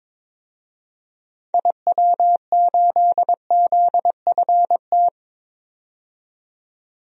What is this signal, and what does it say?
Morse code, heard as a keyed tone, about 22 wpm, 700 Hz. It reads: IW8ZFT
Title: DXCC-100 Call Signs - 6 Characters - 22wpm